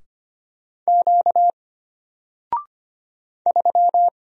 Morse code, heard as a keyed tone, about 25 wpm, 700 Hz. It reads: Q 3
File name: Single-Letter-Number-25wpm